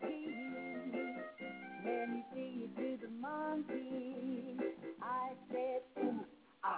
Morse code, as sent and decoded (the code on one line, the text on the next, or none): none